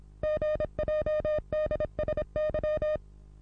Morse code, heard as a keyed tone, about 26 wpm, 610 Hz. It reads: GJDSY
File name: repos / Morsekurs / DJ4UF Morsen27